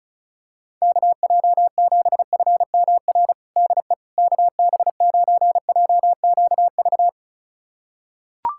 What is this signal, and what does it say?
Morse code, heard as a keyed tone, about 36 wpm, 690 Hz. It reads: KJ7FMR DE KB9JQV
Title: USA Callsign DE Callsign 35wpm